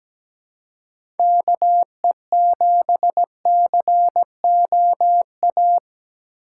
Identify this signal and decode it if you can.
Morse code, heard as a keyed tone, about 17 wpm, 700 Hz. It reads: KE7COA
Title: USA Call Signs - 6 Characters - 2x Character Spacing - 17wpm